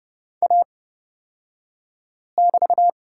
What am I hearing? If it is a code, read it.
Morse code, heard as a keyed tone, about 31 wpm, spaced as for 5 wpm, 700 Hz. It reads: A<BT>
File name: Lesson 36d - BT - Two Characters 30wpm